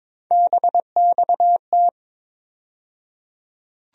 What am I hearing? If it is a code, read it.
Morse code, heard as a keyed tone, about 22 wpm, 700 Hz. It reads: BXT
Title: USA Call Sign Postfixes - 3 Letters 22wpm